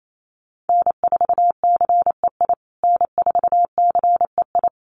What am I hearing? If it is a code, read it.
Morse code, heard as a keyed tone, about 28 wpm, 700 Hz. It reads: N4CEI N4CEI
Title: USA Call Signs Repeated 2x 28wpm